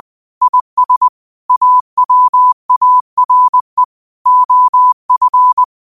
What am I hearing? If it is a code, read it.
Morse code, heard as a keyed tone, about 20 wpm, 1000 Hz. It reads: IS AWARE OF